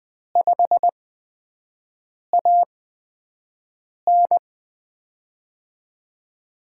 Morse code, heard as a keyed tone, about 20 wpm, 700 Hz. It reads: 5 A N
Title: USA Call Signs - 5 Characters - 8x Character Spacing - 20wpm